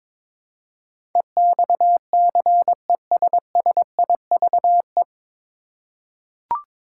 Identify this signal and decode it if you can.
Morse code, heard as a keyed tone, about 22 wpm, 700 Hz. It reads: EXCESSIVE
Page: Morse Code Ninja practice 22wpm